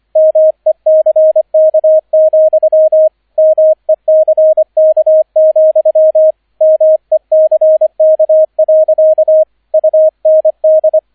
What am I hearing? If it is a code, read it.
Morse code, heard as a keyed tone, about 25 wpm, 630 Hz, with radio noise. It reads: MECK, MECK, MECK. UND
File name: 003 Dritter Streich 25wpm